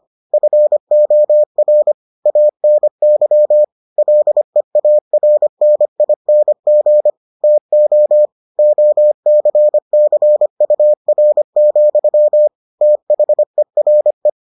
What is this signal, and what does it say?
Morse code, heard as a keyed tone, about 25 wpm, 600 Hz. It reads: FOR ANY LEARNING TO OCCUR, THERE